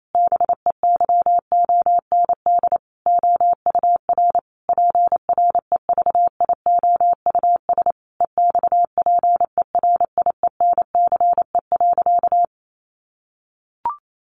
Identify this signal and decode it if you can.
Morse code, heard as a keyed tone, about 28 wpm, 700 Hz. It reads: BEYOND OUR PREVIOUS EXPERIENCE.
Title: Sentences from Top 1500 Words 28wpm